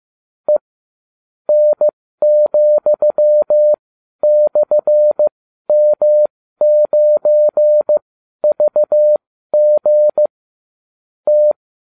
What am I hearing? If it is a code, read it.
Morse code, heard as a keyed tone, about 15 wpm, spaced as for 12 wpm, 600 Hz. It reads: E N,/M9VG T